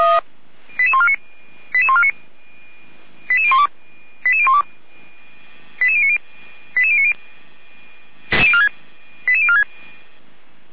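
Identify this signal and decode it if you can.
Five-tone selective call (ZVEI), heard as quick runs of five tones, tones of about 70 ms. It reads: 88149 88149 88012 88012 88090 88090 88046
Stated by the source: ZVEI five-tone selective call received on an SDR, fire brigade Feuerwehr Orlamünde.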